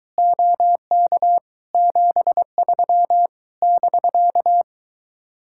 Morse code, heard as a keyed tone, about 23 wpm, 700 Hz. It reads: OK 73 <BK>